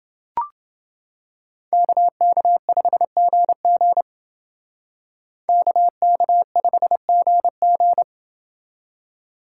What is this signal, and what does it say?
Morse code, heard as a keyed tone, about 28 wpm, 700 Hz. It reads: KK5GG KK5GG